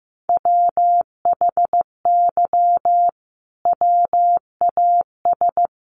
Morse code, heard as a keyed tone, about 15 wpm, 700 Hz. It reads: WHY WAS
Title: Sentences from Top 1400 Words 15wpm